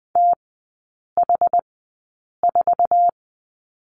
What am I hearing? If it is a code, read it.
Morse code, heard as a keyed tone, about 20 wpm, spaced as for 8 wpm, 700 Hz. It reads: TH4
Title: 074 Lesson 13e - Letter L - Three Characters 20wpm